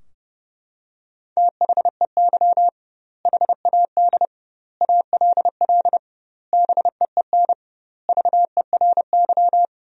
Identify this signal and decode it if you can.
Morse code, heard as a keyed tone, about 32 wpm, 700 Hz. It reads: THEY HAD ALL BEEN VERY
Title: 2x Word-Spacing - Sentences from Top 200 Words - 30wpm